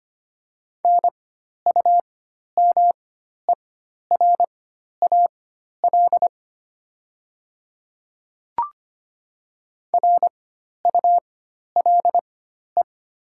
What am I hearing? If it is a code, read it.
Morse code, heard as a keyed tone, about 25 wpm, spaced as for 12 wpm, 700 Hz. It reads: NUMERAL RULE
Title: 4x Character Spacing - Top 500 Words - 25wpm